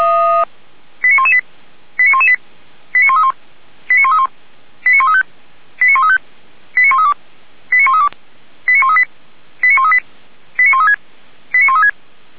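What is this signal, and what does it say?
Five-tone selective call (ZVEI), heard as quick runs of five tones, tones of about 70 ms. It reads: 89118 89118 89131 89131 89135 89135 89132 89132 89138 89138 89136 89136
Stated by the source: Five-tone ZVEI selective call 89107, received by SDR.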